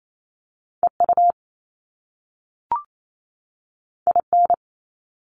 Morse code, heard as a keyed tone, about 28 wpm, 700 Hz. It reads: E U I N
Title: Top Two Letter Combinations in Words 28wpm